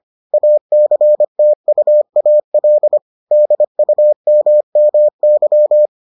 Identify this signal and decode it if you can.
Morse code, heard as a keyed tone, about 25 wpm, 600 Hz. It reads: ACTUAL DUMMY